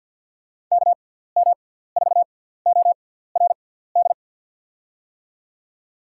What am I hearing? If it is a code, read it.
Morse code, heard as a keyed tone, about 53 wpm, spaced as for 17 wpm, 700 Hz. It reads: KM4ORD
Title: USA Call Signs - 6 Characters - 6x Character Spacing - 50wpm